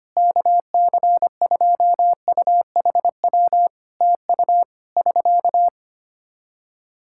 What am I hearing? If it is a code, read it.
Morse code, heard as a keyed tone, about 25 wpm, 700 Hz. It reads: KC2UHW TU <SK>
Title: K1USN SST QSO Practice 25wpm